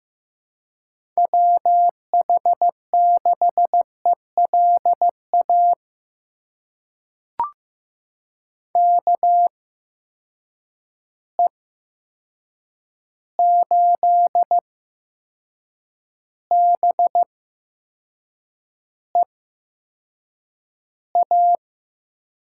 Morse code, heard as a keyed tone, about 15 wpm, spaced as for 6 wpm, 700 Hz. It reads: WH6ELA KE8BEA
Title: USA Call Signs - 6 Characters - 8x Character Spacing - 15wpm